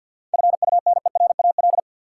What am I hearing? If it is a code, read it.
Morse code, heard as a keyed tone, about 52 wpm, 700 Hz. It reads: FUNERAL
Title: Seven Letter Words - 50wpm